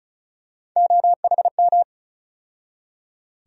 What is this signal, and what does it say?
Morse code, heard as a keyed tone, about 36 wpm, 700 Hz. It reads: OHM
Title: USA Call Sign Postfixes - 3 Letters 35wpm